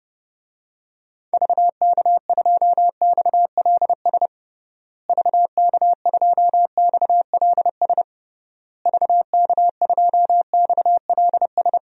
Morse code, heard as a keyed tone, about 30 wpm, 700 Hz. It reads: VK2XLS VK2XLS VK2XLS
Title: DXCC-100 Call Signs - 6 Characters - Repeated 3x - 30wpm